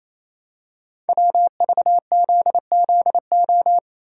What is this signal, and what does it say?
Morse code, heard as a keyed tone, about 29 wpm, 700 Hz. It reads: WVZZO
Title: Code Groups - 5 Letters - 28wpm